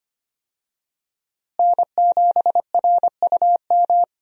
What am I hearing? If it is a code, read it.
Morse code, heard as a keyed tone, about 25 wpm, 700 Hz. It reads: N7RUM
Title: USA Call Signs - 5 Characters - Repeated 2x - 25wpm